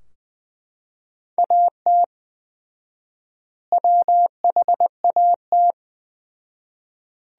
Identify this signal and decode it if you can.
Morse code, heard as a keyed tone, about 20 wpm, 700 Hz. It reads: AT WHAT
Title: 4x Word-Spacing - Sentences from Top 100 Words - 20wpm